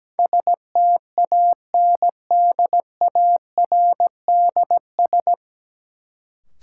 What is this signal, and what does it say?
Morse code, heard as a keyed tone, about 17 wpm, 700 Hz. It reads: STANDARDS
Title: Sentences from Top 1000 Words - Review Part 6 - New Word Drill - 17wpm